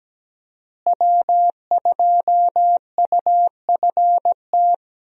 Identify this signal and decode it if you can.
Morse code, heard as a keyed tone, about 17 wpm, 700 Hz. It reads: W2UFT
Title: USA Call Signs - 5 Characters - Repeated 3x - 17wpm